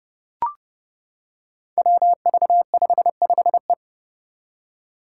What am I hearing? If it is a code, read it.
Morse code, heard as a keyed tone, about 29 wpm, 700 Hz. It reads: WV55E
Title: Five Challenging Letter-Number Combinations 30wpm